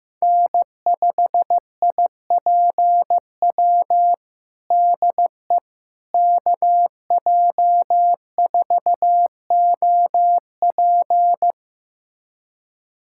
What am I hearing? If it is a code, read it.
Morse code, heard as a keyed tone, about 15 wpm, 700 Hz. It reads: N5IPW DE KJ4OP
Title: USA Callsign DE Callsign - 5 Characters - 15wpm